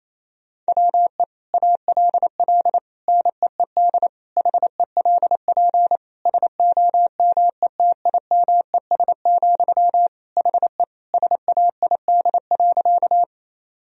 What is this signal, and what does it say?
Morse code, heard as a keyed tone, about 28 wpm, 700 Hz. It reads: WE ALL NEED HELP SOMETIMES, HE SAID.
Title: Sentences from Top 1000 Words - Review Part 3 - 28wpm